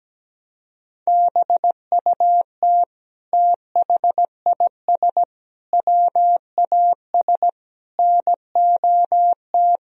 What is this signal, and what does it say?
Morse code, heard as a keyed tone, about 17 wpm, 700 Hz. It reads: BUT THIS WAS NOT
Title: Sentences from Top 1000 Words - Review Part 3 - 17wpm